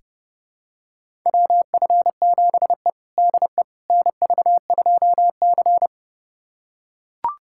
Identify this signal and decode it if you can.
Morse code, heard as a keyed tone, about 30 wpm, 700 Hz. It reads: WF7E DE NV2C
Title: USA Callsign DE Callsign - 4 Characters - 30wpm